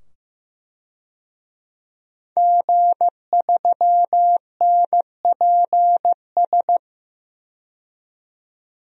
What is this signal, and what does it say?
Morse code, heard as a keyed tone, about 15 wpm, 700 Hz. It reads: G3NPS
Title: DXCC-100 Call Signs - 5 Characters - 15wpm